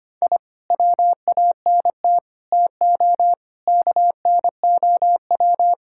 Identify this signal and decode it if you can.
Morse code, heard as a keyed tone, about 25 wpm, 700 Hz. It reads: I WANT TO KNOW
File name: practice / 25wpm / Sentences with 2-Word Phrases - 25wpm